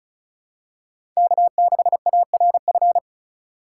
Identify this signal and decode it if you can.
Morse code, heard as a keyed tone, about 36 wpm, 700 Hz. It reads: K6ARF
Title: USA Call Signs - 3x Character Spacing - 35wpm